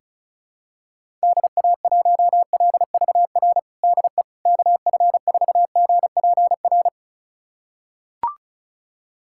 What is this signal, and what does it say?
Morse code, heard as a keyed tone, about 36 wpm, 700 Hz. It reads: DA1LVR DE KF4GPR
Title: USA Callsign DE Callsign - 6 Characters - 35wpm